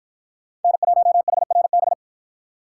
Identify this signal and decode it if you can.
Morse code, heard as a keyed tone, about 53 wpm, 700 Hz. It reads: N1HRB